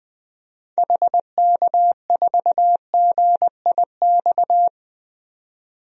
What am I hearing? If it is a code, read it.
Morse code, heard as a keyed tone, about 20 wpm, 700 Hz. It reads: HK4GIX
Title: DXCC-100 Call Signs - 6 Characters - 20wpm